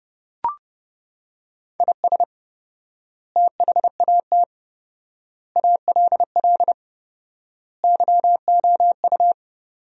Morse code, heard as a keyed tone, about 32 wpm, 700 Hz. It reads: IS THAT ALL YOU
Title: 4x Word-Spacing - Sentences from Top 200 Words - 30wpm